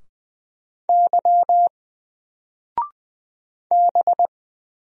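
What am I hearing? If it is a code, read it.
Morse code, heard as a keyed tone, about 20 wpm, 700 Hz. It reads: Y B
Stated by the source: Single letter and number Morse drill at 20 wpm, from N4LCD.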